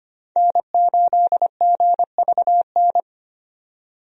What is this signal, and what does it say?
Morse code, heard as a keyed tone, about 25 wpm, 690 Hz. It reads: N8GVN
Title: USA Call Signs - 5 Characters - 25wpm